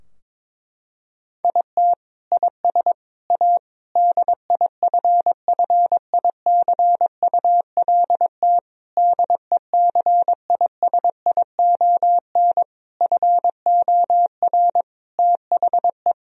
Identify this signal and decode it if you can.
Morse code, heard as a keyed tone, about 22 wpm, 700 Hz. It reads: IT IS A DIFFICULT DECISION FOR THE
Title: Sentences from Top 1000 Words - Review Part 4 - 22wpm